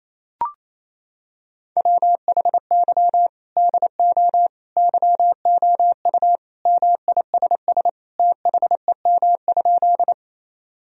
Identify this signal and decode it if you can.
Morse code, heard as a keyed tone, about 28 wpm, 700 Hz. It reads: WHY DO YOU MISS THEM?
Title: Sentences from Top 1000 Words - Review Part 3 - 28wpm